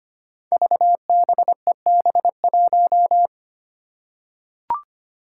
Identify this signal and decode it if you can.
Morse code, heard as a keyed tone, about 25 wpm, 700 Hz. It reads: VBEB1
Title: Five Challenging Letter-Number Combinations 25wpm